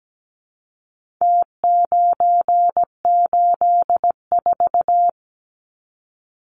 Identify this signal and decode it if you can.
Morse code, heard as a keyed tone, about 17 wpm, 700 Hz. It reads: T984
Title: Numbers - Contest Style - 4-Digit 17wpm